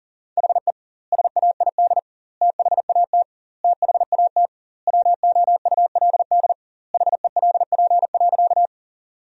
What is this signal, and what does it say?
Morse code, heard as a keyed tone, about 40 wpm, 700 Hz. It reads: HE SAID THAT THAT WOULD HELP.
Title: 2x Word-Spacing - Sentences from Top 200 Words - 40wpm